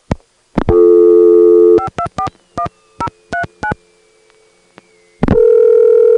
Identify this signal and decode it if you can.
DTMF, heard as telephone touch tones, tones of about 90 ms. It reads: 5241039